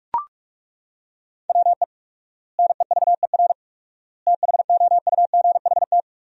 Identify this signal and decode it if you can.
Morse code, heard as a keyed tone, about 41 wpm, 700 Hz. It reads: WE NEVER THOUGHT
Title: 4x Word-Spacing - Sentences from Top 300 Words - 45wpm